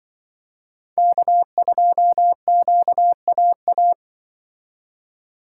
Morse code, heard as a keyed tone, about 23 wpm, 700 Hz. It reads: K2QAA
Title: Speed Racing - USA 1x3 Call Signs 20wpm